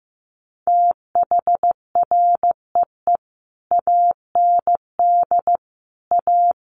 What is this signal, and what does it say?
Morse code, heard as a keyed tone, about 15 wpm, 700 Hz. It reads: THREE AND A
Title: Sets of 5 Words Encore2 15wpm